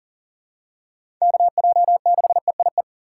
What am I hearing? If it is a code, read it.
Morse code, heard as a keyed tone, about 39 wpm, 700 Hz. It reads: KJ6EIE
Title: USA Call Signs - 6 Characters - 4x Character Spacing - 40wpm